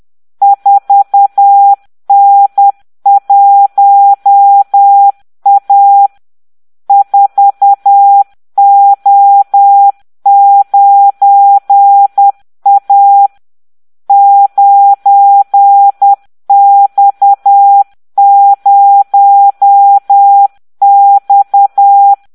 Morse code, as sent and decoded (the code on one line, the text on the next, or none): ....- -. .---- .- / ....- --- ----. .- / ----. -..- ----- -..-
4N1A 4O9A 9X0X